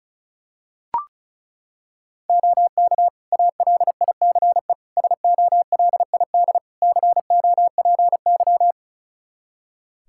Morse code, heard as a keyed tone, about 36 wpm, 700 Hz. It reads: OK ALICE SOLID COPY
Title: Practice QSO - Exchange 3 - More Info - 35wpm